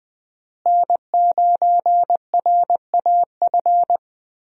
Morse code, heard as a keyed tone, about 20 wpm, 700 Hz. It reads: N9RAF
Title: USA Call Signs - 5 Characters - Repeated 2x - 20wpm